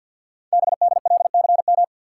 Morse code, heard as a keyed tone, about 52 wpm, 700 Hz. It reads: BDLCM
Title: Code Groups - 5 Letters - 50wpm